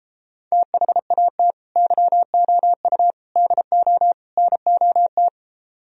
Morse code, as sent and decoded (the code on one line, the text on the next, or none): - .... .- - / -.-- --- ..- / -.. --- / -. --- -
THAT YOU DO NOT